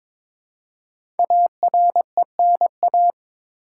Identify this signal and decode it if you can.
Morse code, heard as a keyed tone, about 22 wpm, 700 Hz. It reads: ARENA
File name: 5 Letter Words - First 3 Letters Given - 22wpm